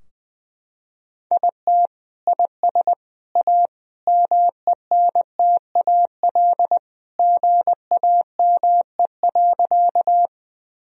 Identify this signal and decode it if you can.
Morse code, heard as a keyed tone, about 20 wpm, 700 Hz. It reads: IT IS A MENTAL GAME.
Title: Sentences from Top 1000 Words - Review Part 4 - 20wpm